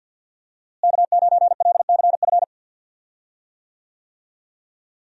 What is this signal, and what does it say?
Morse code, heard as a keyed tone, about 51 wpm, 700 Hz. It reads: K9LCF
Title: USA Call Signs - 5 Characters - Repeated 2x - 50wpm